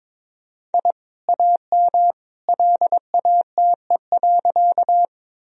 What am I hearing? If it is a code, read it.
Morse code, heard as a keyed tone, about 22 wpm, 700 Hz. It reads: I AM LATE.
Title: Sentences from Top 1000 Words - Review Part 2 - 22wpm